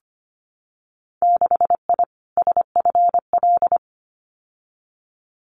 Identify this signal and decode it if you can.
Morse code, heard as a keyed tone, about 25 wpm, 700 Hz. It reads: 6I SFL
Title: Winter Field Day Exchanges 25wpm